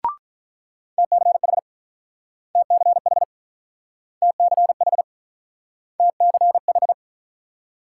Morse code, heard as a keyed tone, about 44 wpm, 690 Hz. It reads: TCH TCH TCH TCH
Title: Speed-Racing Multi-letter Phonemes - 35wpm